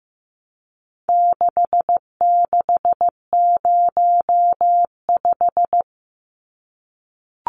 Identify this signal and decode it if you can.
Morse code, heard as a keyed tone, about 15 wpm, 700 Hz. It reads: 6605
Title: Numbers - 4-Digit 15wpm